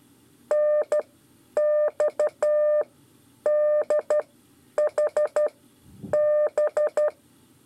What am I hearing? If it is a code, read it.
Morse code, heard as a keyed tone, about 12 wpm, spaced as for 9 wpm, 600 Hz. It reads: NXDHB
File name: 5 LEZIONE RIPASSO 1